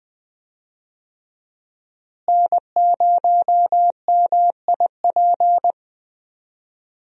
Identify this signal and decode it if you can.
Morse code, heard as a keyed tone, about 20 wpm, 700 Hz. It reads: N0MIP
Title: USA Call Signs - 5 Characters - Repeated 2x - 20wpm